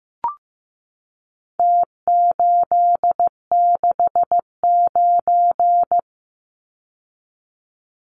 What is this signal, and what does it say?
Morse code, heard as a keyed tone, about 15 wpm, 700 Hz. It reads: T869
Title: Numbers - Contest Style - 4-Digit 15wpm